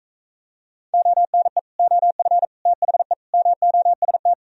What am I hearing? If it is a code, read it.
Morse code, heard as a keyed tone, about 43 wpm, 700 Hz. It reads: ONE OF THE MOST